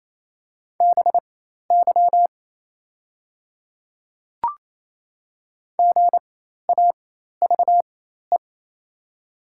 Morse code, heard as a keyed tone, about 28 wpm, spaced as for 13 wpm, 690 Hz. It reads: BY GAVE
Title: 4x Character Spacing - Top 500 Words - 28wpm